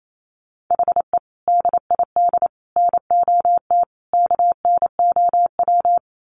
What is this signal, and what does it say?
Morse code, heard as a keyed tone, about 28 wpm, 700 Hz. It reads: HE DID NOT KNOW
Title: Sets of 4 Words 28wpm